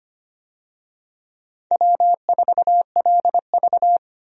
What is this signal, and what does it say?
Morse code, heard as a keyed tone, about 25 wpm, 700 Hz. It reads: W4LV
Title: USA Call Signs - 4 Characters - Repeated 2x - 25wpm